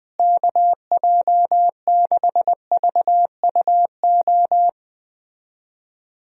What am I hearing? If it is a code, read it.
Morse code, heard as a keyed tone, about 21 wpm, 700 Hz. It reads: KJ6VUO
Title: USA Call Signs - 6 Characters - Repeated 3x - 20wpm